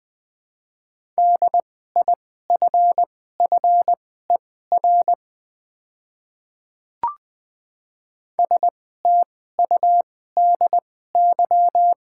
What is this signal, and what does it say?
Morse code, heard as a keyed tone, about 20 wpm, spaced as for 14 wpm, 700 Hz. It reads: DIFFER STUDY
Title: 2x Character Spacing - Top 300 Words - 20wpm